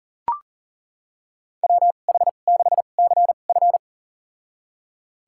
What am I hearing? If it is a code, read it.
Morse code, heard as a keyed tone, about 40 wpm, spaced as for 29 wpm, 700 Hz. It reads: WH6CF
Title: USA Call Signs - 2x Character Spacing - 40wpm